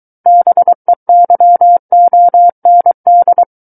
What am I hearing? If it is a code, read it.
Morse code, heard as a keyed tone, about 23 wpm, 700 Hz. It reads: BEYOND
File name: Book One / 14-Chapter 14 - 23 WPM